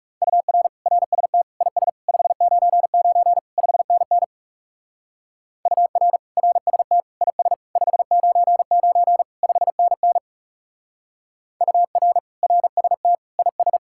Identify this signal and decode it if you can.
Morse code, heard as a keyed tone, about 37 wpm, 700 Hz. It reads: UR RST IS 599 5NN UR RST IS 599 5NN UR RST IS
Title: Speed-Racing Common QSO Phrases - 30wpm